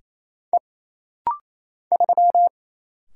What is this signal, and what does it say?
Morse code, heard as a keyed tone, about 28 wpm, 700 Hz. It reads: E 3
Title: Single Letter-Number Alt - Rapid-Fire 28wpm